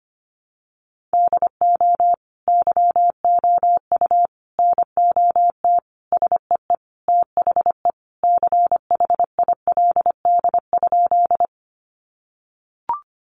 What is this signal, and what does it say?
Morse code, heard as a keyed tone, about 25 wpm, 700 Hz. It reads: DO YOU NOT SEE THE CHILD?
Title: Sentences from Top 800 Words 25wpm